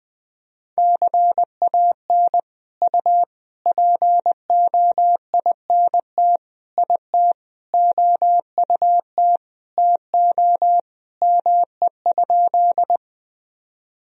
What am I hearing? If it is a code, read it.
Morse code, heard as a keyed tone, about 20 wpm, 700 Hz. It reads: CAN U POINT IT OUT TO ME?
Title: Sentences from Top 200 Words and Abbreviations 20wpm